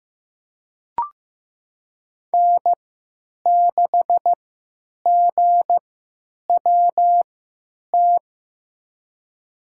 Morse code, heard as a keyed tone, about 15 wpm, spaced as for 9 wpm, 700 Hz. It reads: N6GWT
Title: USA Call Signs - 5 Characters - 3x Character Spacing - 15wpm